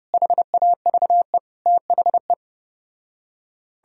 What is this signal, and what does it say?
Morse code, heard as a keyed tone, about 34 wpm, 700 Hz. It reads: HAVE THE